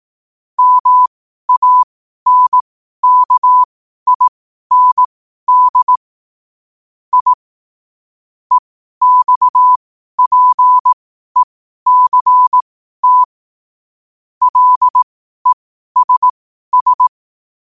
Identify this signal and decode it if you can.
Morse code, heard as a keyed tone, about 18 wpm, spaced as for 12 wpm, 1000 Hz. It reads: MANKIND I EXPECT LESS
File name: qotdfiles / 12wpm